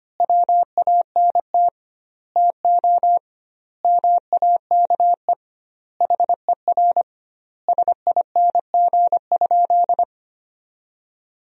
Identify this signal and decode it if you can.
Morse code, heard as a keyed tone, about 25 wpm, 700 Hz. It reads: WANT TO MAKE HER SING?
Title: 2x Word-Spacing - Sentences from Top 400 Words - 25wpm